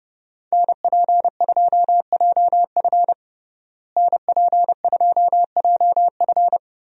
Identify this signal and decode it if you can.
Morse code, heard as a keyed tone, about 30 wpm, 700 Hz. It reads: NP2JF NP2JF
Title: USA Call Signs - 5 Characters - Repeated 2x - 30wpm